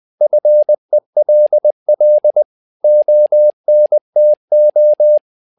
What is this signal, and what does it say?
Morse code, heard as a keyed tone, about 20 wpm, 600 Hz. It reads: FELL ONTO
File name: tower trouble 20wpm